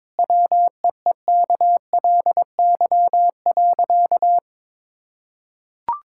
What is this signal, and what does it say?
Morse code, heard as a keyed tone, about 22 wpm, 700 Hz. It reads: WEEKLY.
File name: practice / 22wpm / Easy Sentences with Hints - 22wpm